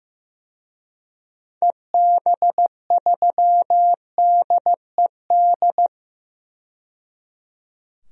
Morse code, heard as a keyed tone, about 15 wpm, 700 Hz. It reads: EB3DED
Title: DXCC-100 Call Signs - 6 Characters - 15wpm